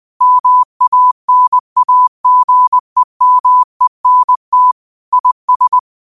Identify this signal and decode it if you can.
Morse code, heard as a keyed tone, about 20 wpm, 1000 Hz. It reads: MANAGEMENT IS
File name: qotdfiles / 20wpm